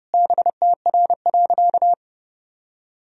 Morse code, heard as a keyed tone, about 32 wpm, 700 Hz. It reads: BTR.
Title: Sentences from Top 400 Words and Abbreviations 30wpm